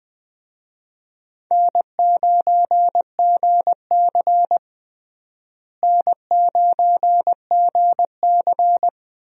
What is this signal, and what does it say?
Morse code, heard as a keyed tone, about 20 wpm, 700 Hz. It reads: N9GC N9GC
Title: USA Call Signs - 4 Characters - Repeated 3x - 20wpm